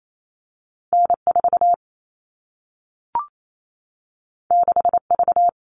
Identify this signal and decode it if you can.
Morse code, heard as a keyed tone, about 28 wpm, 700 Hz. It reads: N4 6V
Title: All Cut Numbers - 2-Digit 28wpm